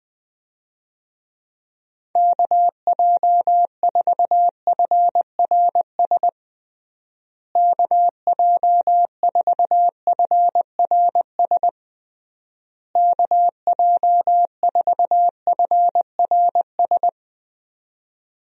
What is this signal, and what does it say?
Morse code, heard as a keyed tone, about 20 wpm, 700 Hz. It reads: KJ4FRS KJ4FRS KJ4FRS
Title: USA Call Signs - 6 Characters - Repeated 3x - 20wpm